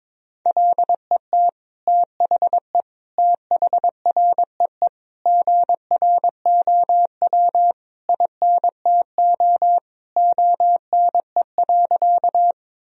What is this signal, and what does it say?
Morse code, heard as a keyed tone, about 22 wpm, 700 Hz. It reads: LET THE THREE GROW INTO ONE.
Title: Sentences from Top 1000 Words - Review Part 1 - 22wpm